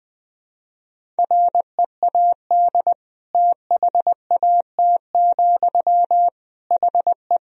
Morse code, heard as a keyed tone, about 20 wpm, 700 Hz. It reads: READ THAT, HE
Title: Sentences from Top 200 Words and Abbreviations 20wpm